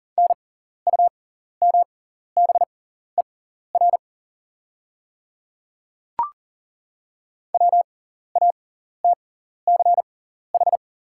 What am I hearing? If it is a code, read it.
Morse code, heard as a keyed tone, about 40 wpm, spaced as for 14 wpm, 700 Hz. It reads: NUMBER WATCH
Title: 6x Character Spacing - Top 300 Words - 40wpm